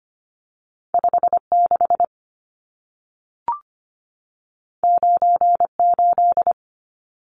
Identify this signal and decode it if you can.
Morse code, heard as a keyed tone, about 25 wpm, 700 Hz. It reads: 56 98
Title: Numbers - 2-Digit 25wpm